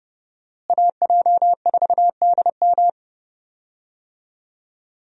Morse code, heard as a keyed tone, about 30 wpm, 700 Hz. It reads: AJ4DM
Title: USA Call Signs - 5 Characters - Repeated 2x - 30wpm